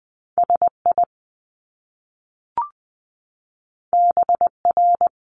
Morse code, heard as a keyed tone, about 20 wpm, 700 Hz. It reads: SI BR
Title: Top Two Letter Combinations in Words 20wpm